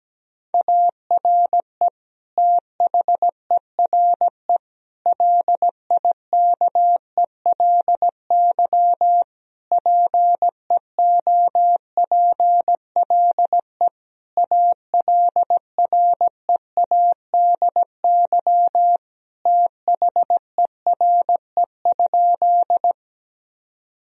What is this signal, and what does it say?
Morse code, heard as a keyed tone, about 17 wpm, 700 Hz. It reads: ARE THERE LIKELY PEOPLE ALREADY THERE?
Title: Sentences from Top 1000 Words - Review Part 4 - 17wpm